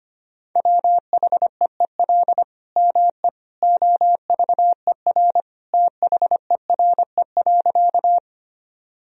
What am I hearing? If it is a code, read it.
Morse code, heard as a keyed tone, about 25 wpm, 700 Hz. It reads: WHEEL ME OVER THERE.
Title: Sentences from Top 1000 Words - Review Part 3 - 25wpm